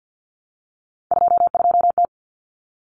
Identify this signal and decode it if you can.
Morse code, heard as a keyed tone, about 50 wpm, 700 Hz. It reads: 22T
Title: Common Cut Numbers - 3-Digit 50wpm